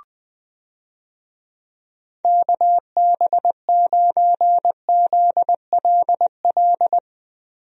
Morse code, heard as a keyed tone, about 20 wpm, 700 Hz. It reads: KB9ZLL
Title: USA Call Signs - 6 Characters - Repeated 3x - 20wpm